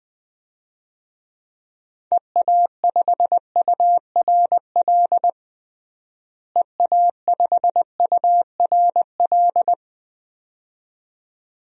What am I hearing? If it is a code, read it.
Morse code, heard as a keyed tone, about 20 wpm, 700 Hz. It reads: EA5URL EA5URL
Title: DXCC-100 Call Signs - 6 Characters - Repeated 2x - 20wpm